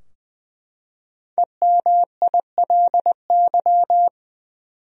Morse code, heard as a keyed tone, about 20 wpm, 700 Hz. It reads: EMILY
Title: USA Common Names - 20wpm